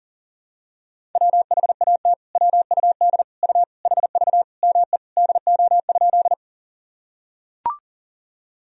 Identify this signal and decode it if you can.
Morse code, heard as a keyed tone, about 40 wpm, 700 Hz. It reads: WHAT WUD U HV ME DO?
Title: Sentences from Top 200 Words and Abbreviations 40wpm